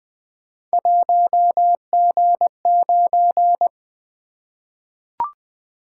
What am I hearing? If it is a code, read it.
Morse code, heard as a keyed tone, about 20 wpm, 700 Hz. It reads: 1G9
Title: Three Letter-Number Combinations - 20wpm